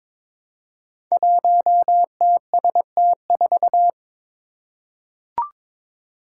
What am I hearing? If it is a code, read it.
Morse code, heard as a keyed tone, about 23 wpm, 700 Hz. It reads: 1TST4
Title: Five Challenging Letter-Number Combinations 22wpm